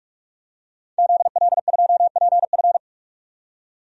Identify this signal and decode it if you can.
Morse code, heard as a keyed tone, about 46 wpm, 700 Hz. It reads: ZL2PF
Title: DXCC-100 Call Signs - 5 Characters - 45wpm